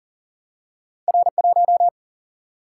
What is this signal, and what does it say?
Morse code, heard as a keyed tone, about 38 wpm, 700 Hz. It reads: R1
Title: Single-Letter Single-Number 40wpm